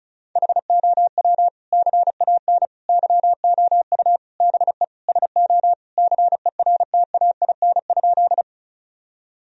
Morse code, heard as a keyed tone, about 34 wpm, 700 Hz. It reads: HOW CAN YOU BE SO CERTAIN?
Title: Sentences from Top 1000 Words - Review Part 2 - 35wpm